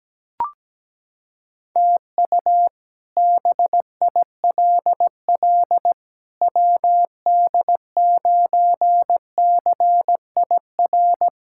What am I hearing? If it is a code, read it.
Morse code, heard as a keyed tone, about 17 wpm, 700 Hz. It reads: TU BILL WD9CIR
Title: K1USN SST QSO Practice - 17wpm